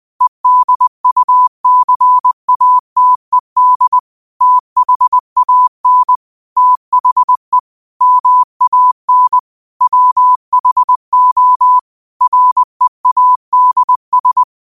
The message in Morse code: EDUCATED THAN THE MAN WHO READS

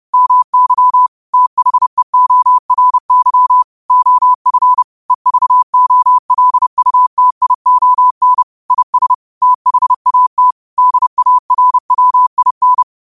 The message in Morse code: MY THEORY OF EVOLUTION IS THAT DARWIN

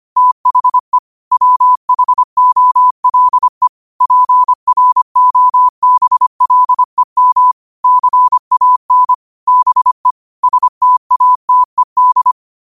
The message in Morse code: THE WHOLE PROBLEM CAN BE STATED

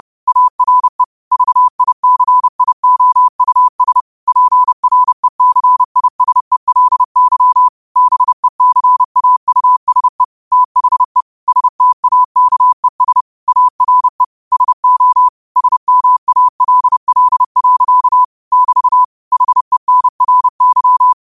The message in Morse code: ARE VICIOUS PRECISELY BECAUSE THE STAKES ARE SO SMALL. <BT> HENRY

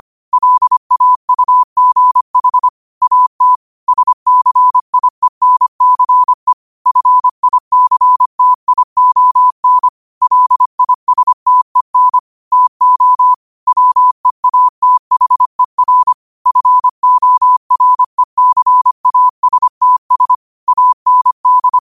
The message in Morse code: LAUGH AT SCIENCE FICTION LISTEN TO WEATHER FORECASTS AND

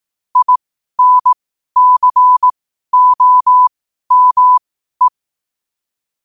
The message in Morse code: INCOME

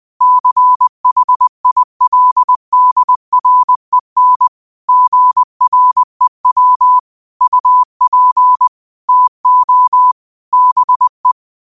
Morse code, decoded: CHILDREN GREW UP TO BE

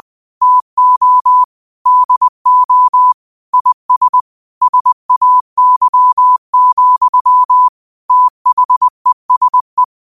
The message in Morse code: TO DO IS SAY, THESE

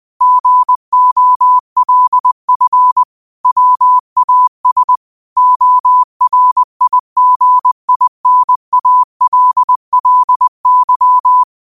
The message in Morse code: GOLF WAS ORIGINALLY

